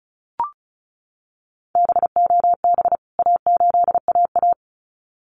Morse code, heard as a keyed tone, about 34 wpm, 700 Hz. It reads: BOB A8AA
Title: CWT Exchange 35wpm